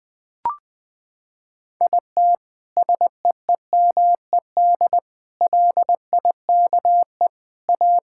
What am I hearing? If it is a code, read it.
Morse code, heard as a keyed tone, about 20 wpm, 700 Hz. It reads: IT SEEMED LIKE A
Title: Sentences with 9-Word Phrases - 20wpm